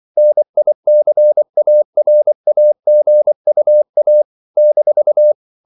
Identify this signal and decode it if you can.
Morse code, heard as a keyed tone, about 24 wpm, 600 Hz. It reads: NICARAGUA -
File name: CW / Practice / NationalCapitals24